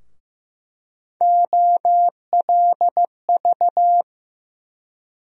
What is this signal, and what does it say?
Morse code, heard as a keyed tone, about 15 wpm, 700 Hz. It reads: OLV
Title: USA Call Sign Postfixes - 3 Letters 15wpm